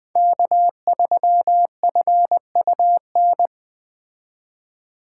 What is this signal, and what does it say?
Morse code, heard as a keyed tone, about 21 wpm, 700 Hz. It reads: K3FUN
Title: Speed Racing - USA 1x3 Call Signs 17wpm